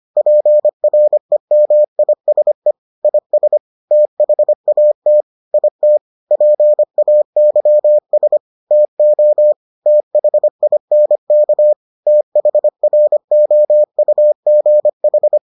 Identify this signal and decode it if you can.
Morse code, heard as a keyed tone, about 25 wpm, 600 Hz. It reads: PREMISE IS THAT IT PAYS TO THINK THROUGH